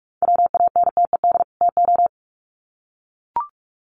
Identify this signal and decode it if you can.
Morse code, heard as a keyed tone, about 44 wpm, 700 Hz. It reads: WANTED TO
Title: Sets of 2 Words 45wpm